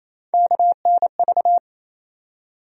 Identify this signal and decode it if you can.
Morse code, heard as a keyed tone, about 28 wpm, 700 Hz. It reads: KNV
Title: USA Call Sign Postfixes - 3 Letters 28wpm